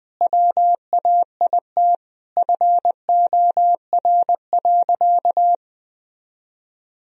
Morse code, heard as a keyed tone, about 20 wpm, 700 Hz. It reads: WAIT FOR.
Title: Sentences from Top 1000 Words - Review Part 3 - 20wpm